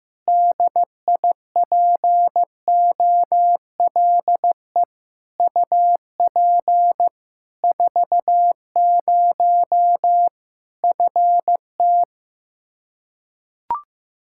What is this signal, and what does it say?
Morse code, heard as a keyed tone, about 15 wpm, 700 Hz. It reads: DIPOLE UP 40 FT
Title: Practice QSO - Exchange 2 - Info - 15wpm